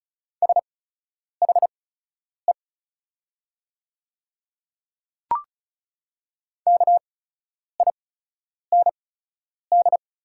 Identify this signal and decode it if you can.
Morse code, heard as a keyed tone, about 34 wpm, spaced as for 10 wpm, 690 Hz. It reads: SHE KIND
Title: 8x Character Spacing - Top 200 Words - 35wpm